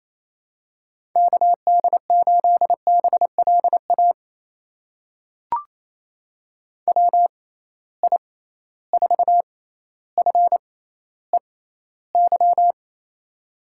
Morse code, heard as a keyed tone, about 28 wpm, spaced as for 14 wpm, 700 Hz. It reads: KD8BLA WI4FEY
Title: USA Call Signs - 6 Characters - 6x Character Spacing - 28wpm